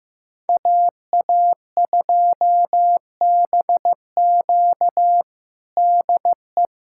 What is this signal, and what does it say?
Morse code, heard as a keyed tone, about 15 wpm, 700 Hz. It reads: AA2BQ DE
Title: USA Callsign DE Callsign - 5 Characters - 15wpm